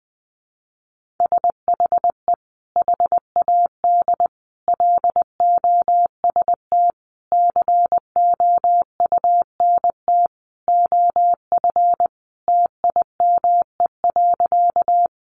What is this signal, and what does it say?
Morse code, heard as a keyed tone, about 20 wpm, 700 Hz. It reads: SHE HAD LOST COUNT OF TIME.
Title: Sentences from Top 1500 Words 20wpm